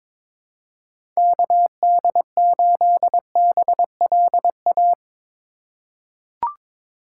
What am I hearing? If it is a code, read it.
Morse code, heard as a keyed tone, about 22 wpm, 700 Hz. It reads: KD8BLA
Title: USA Call Signs - 6 Characters - 2x Character Spacing - 22wpm